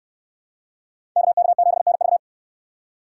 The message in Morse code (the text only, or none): KK7NV